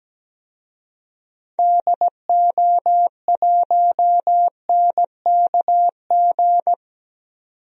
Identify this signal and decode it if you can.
Morse code, heard as a keyed tone, about 17 wpm, 700 Hz. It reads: DO1NKG
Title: DXCC-100 Call Signs - 6 Characters - 17wpm